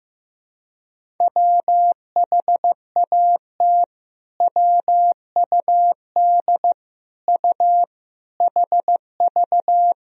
Morse code, heard as a keyed tone, about 15 wpm, 700 Hz. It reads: WHAT WUD U HV
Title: Sentences from Top 300 Words and Abbreviations 15wpm